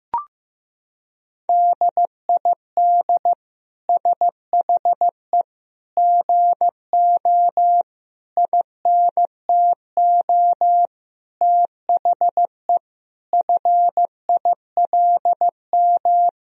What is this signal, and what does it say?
Morse code, heard as a keyed tone, about 16 wpm, 700 Hz. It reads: DID SHE GO INTO THE FILM
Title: Sentences from Top 1000 Words - Review Part 3 - 15wpm